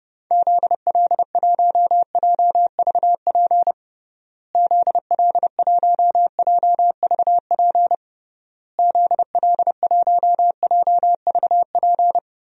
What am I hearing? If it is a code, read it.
Morse code, heard as a keyed tone, about 32 wpm, 700 Hz. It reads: ZL1JVP ZL1JVP ZL1JVP
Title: DXCC-100 Call Signs - 6 Characters - Repeated 3x - 30wpm